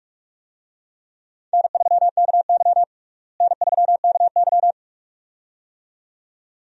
Morse code, heard as a keyed tone, about 46 wpm, 700 Hz. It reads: N3KY N3KY
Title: USA Call Signs - 4 Characters - Repeated 2x - 45wpm